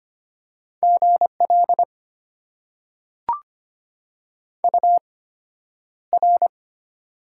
Morse code, heard as a keyed tone, about 25 wpm, spaced as for 11 wpm, 700 Hz. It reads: GL UR
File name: Top 90 QSO Elements - 8x Character Spacing - 25wpm